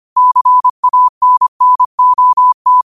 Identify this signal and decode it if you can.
Morse code, heard as a keyed tone, about 25 wpm, 1000 Hz. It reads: CANNOT
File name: qotdfiles / 25wpm